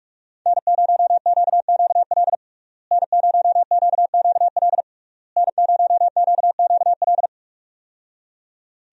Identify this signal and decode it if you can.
Morse code, heard as a keyed tone, about 46 wpm, 700 Hz. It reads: N0QQL N0QQL N0QQL
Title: USA Call Signs - 5 Characters - Repeated 3x - 45wpm